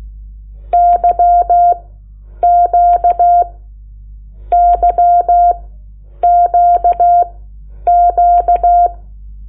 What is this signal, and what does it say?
Morse code, heard as a keyed tone, about 16 wpm, spaced as for 8 wpm, 700 Hz. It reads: YQYQQ
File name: Lecciones / CLASE Nº8